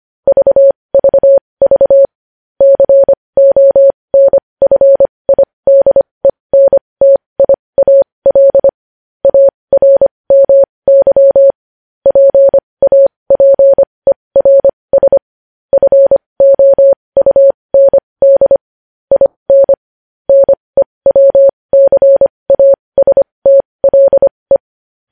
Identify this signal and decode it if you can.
Morse code, heard as a keyed tone, about 25 wpm, spaced as for 20 wpm, 550 Hz. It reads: VVV CONFIDENTIAL ARMY PAPERS FOUND IN NEWCASTLE
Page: news-headlines-in-morse-code-at-20-wpm